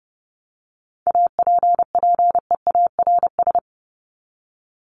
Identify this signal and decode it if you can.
Morse code, heard as a keyed tone, about 30 wpm, 700 Hz. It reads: APPEARS
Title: ES:S Suffix - Top 100 Words with Base Words 30wpm